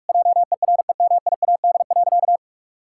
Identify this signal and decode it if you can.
Morse code, heard as a keyed tone, about 46 wpm, 700 Hz. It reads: JEREMIAD.
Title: Pangram Challenge v2 - 45wpm